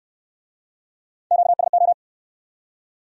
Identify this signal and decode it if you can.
Morse code, heard as a keyed tone, about 62 wpm, 700 Hz. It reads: QSO